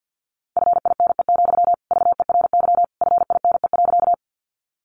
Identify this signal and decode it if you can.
Morse code, heard as a keyed tone, about 50 wpm, 700 Hz. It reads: FINE, VERY FINE.
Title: Sentences from Top 500 Words 50wpm